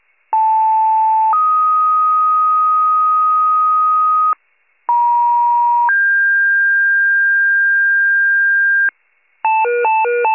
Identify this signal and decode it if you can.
Two-tone page, heard as a one-second tone, then a three-second tone, about 880 then 1250 Hz, about 950 then 1600 Hz, with noise. It is heard from 0.3 to 8.9 s.
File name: pager / 83 RESCUE DUTY / wcalert